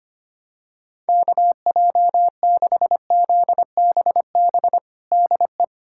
Morse code, heard as a keyed tone, about 25 wpm, 700 Hz. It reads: KJ6ZBB DE
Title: USA Callsign DE Callsign 25wpm